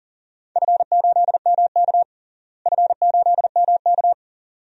Morse code, heard as a keyed tone, about 40 wpm, 700 Hz. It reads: F8MK F8MK
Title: DXCC-100 Call Signs - 4 Characters - Repeated 2x - 40wpm